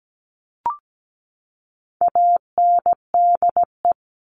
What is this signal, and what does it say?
Morse code, heard as a keyed tone, about 17 wpm, 700 Hz. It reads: ANDE
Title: All Cut Numbers - 4-Digit 17wpm